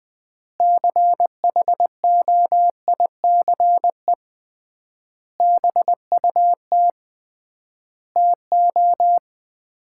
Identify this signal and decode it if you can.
Morse code, heard as a keyed tone, about 20 wpm, 700 Hz. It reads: CHOICE BUT TO
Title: 3x Word-Spacing - Sets of 5 Words - 20wpm